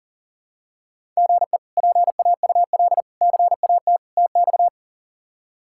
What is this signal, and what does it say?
Morse code, heard as a keyed tone, about 40 wpm, 700 Hz. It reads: GE PAUL CAT TX
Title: K1USN SST QSO Practice - 40wpm